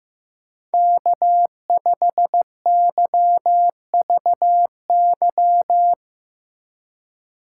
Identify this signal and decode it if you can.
Morse code, heard as a keyed tone, about 15 wpm, 700 Hz. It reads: K5YVY
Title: K1USN SST QSO Practice - 15wpm